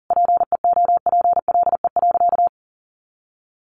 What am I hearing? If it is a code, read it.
Morse code, heard as a keyed tone, about 40 wpm, 700 Hz. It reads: PEOPLE.
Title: Sentences from Top 500 Words 40wpm